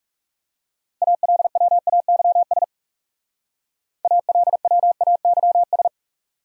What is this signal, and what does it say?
Morse code, heard as a keyed tone, about 36 wpm, 700 Hz. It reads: ALWAYS ALWAYS
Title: Speed-Racing Top 300 Words 30wpm